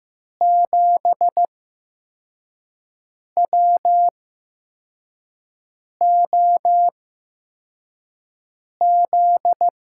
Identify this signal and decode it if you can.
Morse code, heard as a keyed tone, about 15 wpm, spaced as for 4 wpm, 700 Hz. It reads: 7WOZ
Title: USA Call Signs - 6 Characters - 8x Character Spacing - 15wpm